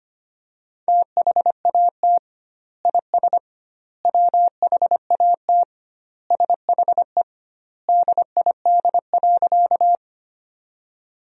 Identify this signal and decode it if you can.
Morse code, heard as a keyed tone, about 25 wpm, 700 Hz. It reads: THAT IS WHAT SHE DID.
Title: 2x Word-Spacing - Sentences from Top 100 Words - 25wpm